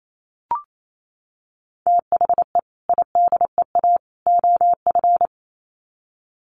Morse code, heard as a keyed tone, about 28 wpm, 700 Hz. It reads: THE IDEA OF
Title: Sets of 3 Words 28wpm